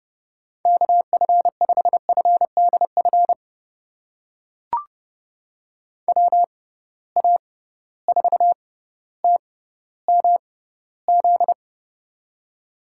Morse code, heard as a keyed tone, about 30 wpm, spaced as for 15 wpm, 700 Hz. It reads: KF5FDF WA4TMZ
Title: USA Call Signs - 6 Characters - 6x Character Spacing - 30wpm